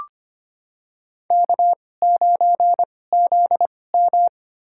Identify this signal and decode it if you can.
Morse code, heard as a keyed tone, about 25 wpm, spaced as for 18 wpm, 700 Hz. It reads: K9ZM
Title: USA Call Signs - 4 Characters - 2x Character Spacing - 25wpm